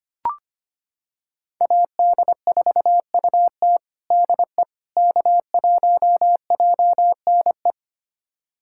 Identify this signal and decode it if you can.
Morse code, heard as a keyed tone, about 25 wpm, 700 Hz. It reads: AD4UT DE K1JNE
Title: USA Callsign DE Callsign - 5 Characters - 25wpm